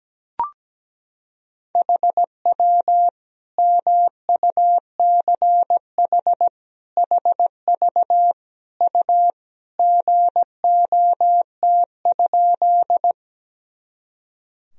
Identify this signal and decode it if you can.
Morse code, heard as a keyed tone, about 17 wpm, 700 Hz. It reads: HW MUCH HV U GOT?
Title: Sentences from Top 300 Words and Abbreviations 17wpm